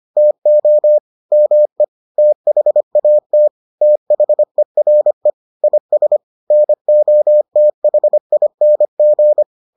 Morse code, heard as a keyed tone, about 25 wpm, 600 Hz. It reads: TO ME THAT THERE IS NOTHING